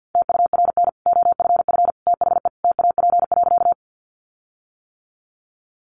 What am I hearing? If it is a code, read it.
Morse code, heard as a keyed tone, about 49 wpm, 700 Hz. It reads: TURN OFF THE TAP.
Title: Sentences from Top 1400 Words 50wpm